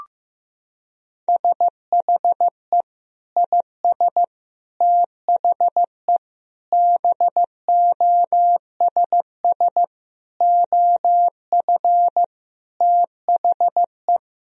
Morse code, heard as a keyed tone, about 15 wpm, 700 Hz. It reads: SHE IS THE BOSS OF THE
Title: Sentences from Top 1000 Words - Review Part 4 - 15wpm